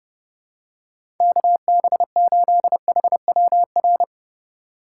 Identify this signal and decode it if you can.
Morse code, heard as a keyed tone, about 32 wpm, 700 Hz. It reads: KB8HWR
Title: USA Call Signs - 6 Characters - 2x Character Spacing - 30wpm